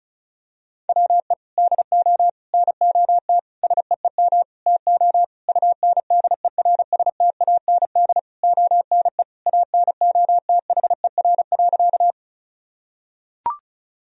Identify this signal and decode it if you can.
Morse code, heard as a keyed tone, about 34 wpm, 700 Hz. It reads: WE DO NOT SEEM TO UNDERSTAND ONE ANOTHER.
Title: Sentences from Top 1000 Words - Review Part 1 - 35wpm